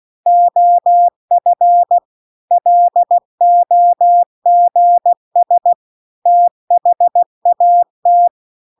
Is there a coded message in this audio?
Yes: Morse code from 0.3 to 8.3 s, heard as a keyed tone, about 16 wpm, 700 Hz. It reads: OF LOGS THAT